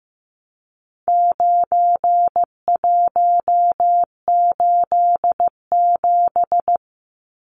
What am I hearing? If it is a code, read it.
Morse code, heard as a keyed tone, about 15 wpm, 700 Hz. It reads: 9187
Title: Numbers - 4-Digit 15wpm